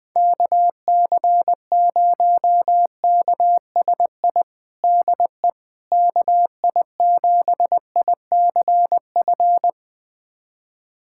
Morse code, heard as a keyed tone, about 20 wpm, 700 Hz. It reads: KC0KSI DE KI7ICF
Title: USA Callsign DE Callsign - 6 Characters - 20wpm